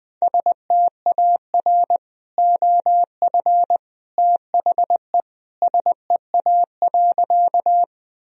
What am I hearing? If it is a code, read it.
Morse code, heard as a keyed tone, about 20 wpm, 700 Hz. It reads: STAR OF THE SEA.
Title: Sentences from Top 1000 Words - Review Part 2 - 20wpm